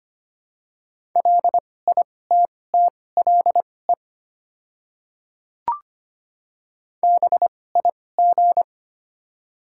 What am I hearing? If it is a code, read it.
Morse code, heard as a keyed tone, about 25 wpm, spaced as for 18 wpm, 700 Hz. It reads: LITTLE BIG
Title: 2x Character Spacing - Top 200 Words - 25wpm